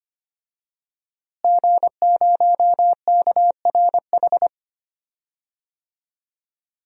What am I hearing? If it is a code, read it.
Morse code, heard as a keyed tone, about 25 wpm, 700 Hz. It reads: G0KRH
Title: DXCC-100 Call Signs - 5 Characters - Repeated 2x - 25wpm